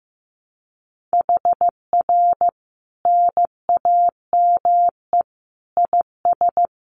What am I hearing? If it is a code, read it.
Morse code, heard as a keyed tone, about 15 wpm, 700 Hz. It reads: HR NAME IS
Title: Common QSO Phrases 15wpm